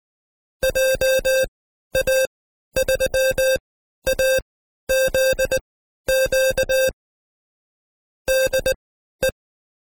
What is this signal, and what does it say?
Morse code, heard as a keyed tone, about 19 wpm, spaced as for 12 wpm, 530 Hz. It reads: JA3AZQ DE
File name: Code Quick / AudioDownld2 / Seg132-Stepper5 11WPM